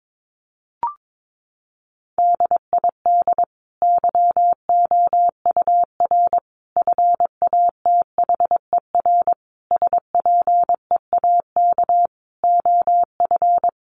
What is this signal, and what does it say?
Morse code, heard as a keyed tone, about 22 wpm, 700 Hz. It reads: DID YOUR FATHER SPEAK OF
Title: Sentences from Top 1100 Words 22wpm